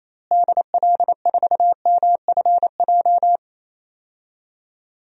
Morse code, heard as a keyed tone, about 28 wpm, 700 Hz. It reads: DL4MFJ
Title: DXCC-100 Call Signs - 6 Characters - Repeated 3x - 28wpm